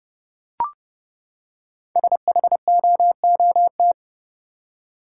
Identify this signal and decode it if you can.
Morse code, heard as a keyed tone, about 29 wpm, 700 Hz. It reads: SHOOT
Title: Five Letter Words - 30wpm